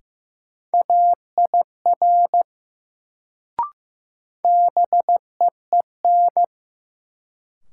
Morse code, heard as a keyed tone, about 15 wpm, 700 Hz. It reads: AIR BEEN
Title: Sets of 2 Words - New Word Drill - 15wpm